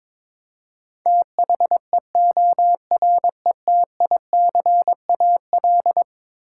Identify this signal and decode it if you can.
Morse code, heard as a keyed tone, about 22 wpm, 700 Hz. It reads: THEORETICAL
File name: practice / 22wpm / Eleven Letter Words - 22wpm